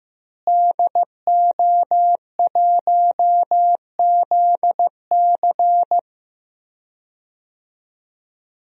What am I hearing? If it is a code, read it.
Morse code, heard as a keyed tone, about 15 wpm, 700 Hz. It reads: DO1ZC
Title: DXCC-100 Call Signs - 5 Characters - Repeated 3x - 15wpm